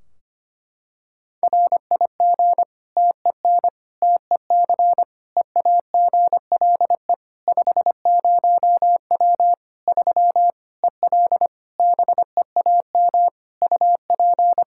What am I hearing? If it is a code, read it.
Morse code, heard as a keyed tone, about 25 wpm, 700 Hz. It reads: RIG TEN TEC EAGLE 50W 3 EL BEAM UP
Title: Practice QSO - Exchange 2 - Info - 25wpm